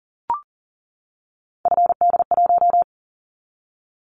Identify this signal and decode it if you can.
Morse code, heard as a keyed tone, about 42 wpm, 700 Hz. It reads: FD1
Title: Callsign Prefixes for Top 50 DXCC - 40wpm